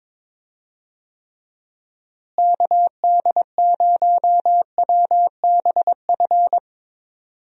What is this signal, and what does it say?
Morse code, heard as a keyed tone, about 21 wpm, 700 Hz. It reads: KD0WBF
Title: USA Call Signs - 6 Characters - Repeated 3x - 22wpm